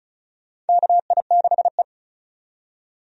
Morse code, heard as a keyed tone, about 36 wpm, 700 Hz. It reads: KI6E
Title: USA Call Signs - 4 Characters - 35wpm